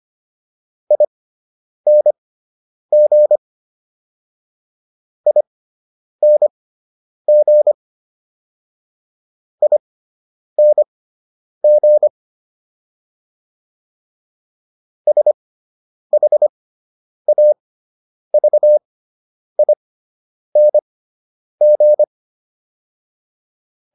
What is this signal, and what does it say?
Morse code, heard as a keyed tone, about 25 wpm, spaced as for 9 wpm, 600 Hz. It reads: ING ING ING SHAVING